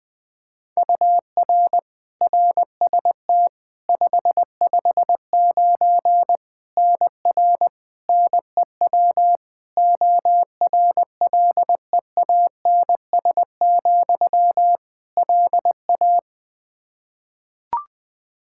Morse code, heard as a keyed tone, about 20 wpm, 700 Hz. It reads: UR RST 559 NR NEW ORLEANS, LA
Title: K1USN SST QSO Practice 20wpm